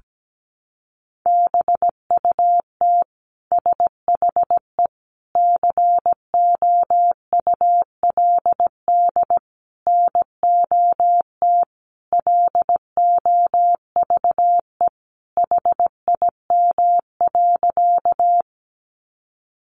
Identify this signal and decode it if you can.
Morse code, heard as a keyed tone, about 17 wpm, 700 Hz. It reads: BUT SHE COULD NOT LOVE HIM.
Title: Sentences from Top 600 Words 17wpm